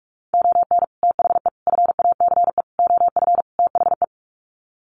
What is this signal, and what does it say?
Morse code, heard as a keyed tone, about 44 wpm, 700 Hz. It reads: ON THE FACE OF THE
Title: Sets of 5 Words Encore2 45wpm